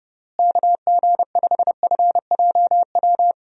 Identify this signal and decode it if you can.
Morse code, heard as a keyed tone, about 30 wpm, 700 Hz. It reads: KG5FJW
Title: USA Call Signs - 6 Characters - Repeated 2x - 30wpm